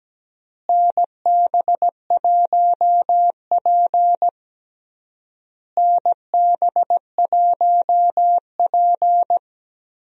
Morse code, heard as a keyed tone, about 17 wpm, 700 Hz. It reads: NB1P NB1P
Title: USA Call Signs - 4 Characters - Repeated 3x - 17wpm